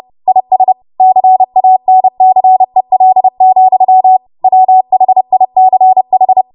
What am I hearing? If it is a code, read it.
Morse code, heard as a keyed tone, about 30 wpm, 750 Hz. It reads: IS CANCEL, WHICH
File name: Morse / 30WPM